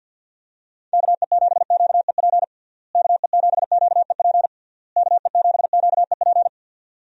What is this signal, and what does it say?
Morse code, heard as a keyed tone, about 52 wpm, 700 Hz. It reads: KE7QEP KE7QEP KE7QEP
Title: USA Call Signs - 6 Characters - Repeated 3x - 50wpm